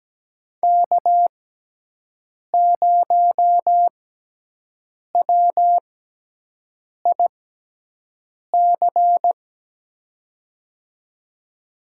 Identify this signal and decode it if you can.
Morse code, heard as a keyed tone, about 17 wpm, spaced as for 6 wpm, 700 Hz. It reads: K0WIC
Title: USA Call Signs - 5 Characters - 6x Character Spacing - 17wpm